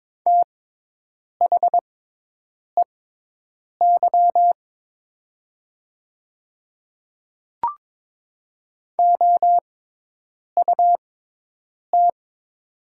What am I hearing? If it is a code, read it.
Morse code, heard as a keyed tone, about 22 wpm, spaced as for 8 wpm, 700 Hz. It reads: THEY OUT
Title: 6x Character Spacing - Top 100 Words - 22wpm